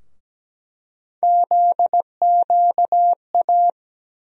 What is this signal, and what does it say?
Morse code, heard as a keyed tone, about 17 wpm, 700 Hz. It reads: ZQA
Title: USA Call Sign Postfixes - 3 Letters 17wpm